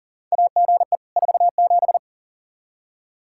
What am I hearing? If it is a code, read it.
Morse code, heard as a keyed tone, about 40 wpm, 700 Hz. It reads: AGE 47
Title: Practice QSO - Exchange 3 - More Info - 40wpm